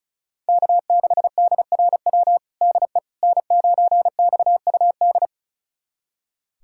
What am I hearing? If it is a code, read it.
Morse code, heard as a keyed tone, about 36 wpm, 700 Hz. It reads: K6DRW DE N9XUD
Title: USA Callsign DE Callsign - 5 Characters - 35wpm